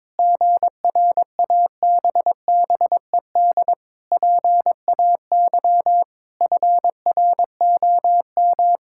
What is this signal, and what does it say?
Morse code, heard as a keyed tone, about 22 wpm, 700 Hz. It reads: GRABBED PAY FROM
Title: Pangram Challenge v2 - 22wpm